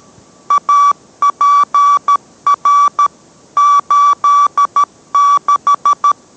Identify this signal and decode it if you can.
Morse code, heard as a keyed tone, about 13 wpm, 1150 Hz. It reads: APR 86